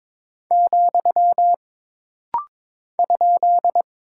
Morse code, heard as a keyed tone, about 22 wpm, 700 Hz. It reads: , ?
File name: practice / 22wpm / Single Punctuation-Prosign - Rapid-Fire - 22wpm